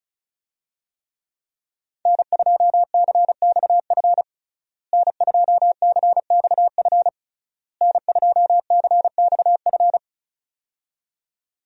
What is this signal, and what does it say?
Morse code, heard as a keyed tone, about 34 wpm, 700 Hz. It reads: N2CXF N2CXF N2CXF
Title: USA Call Signs - 5 Characters - Repeated 3x - 35wpm